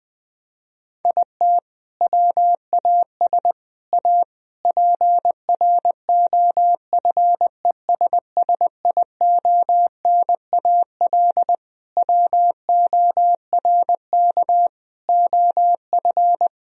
Morse code, heard as a keyed tone, about 20 wpm, 700 Hz. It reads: IT WAS A PROFESSIONAL WORK OF
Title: Sentences from Top 1000 Words - Review Part 4 - 20wpm